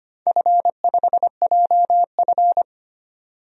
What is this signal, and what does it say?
Morse code, heard as a keyed tone, about 25 wpm, 700 Hz. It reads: F5JF
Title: DXCC-100 Call Signs - 4 Characters - 25wpm